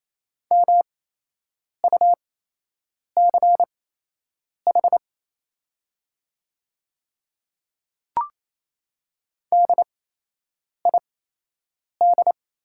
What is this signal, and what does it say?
Morse code, heard as a keyed tone, about 27 wpm, spaced as for 8 wpm, 700 Hz. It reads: MUCH DID
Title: 8x Character Spacing - Top 200 Words - 28wpm